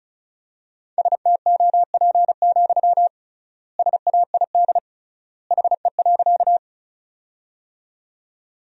Morse code, heard as a keyed tone, about 36 wpm, 700 Hz. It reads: STOP, SAID HE.
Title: 3x Word-Spacing - Sentences from Top 300 Words - 35wpm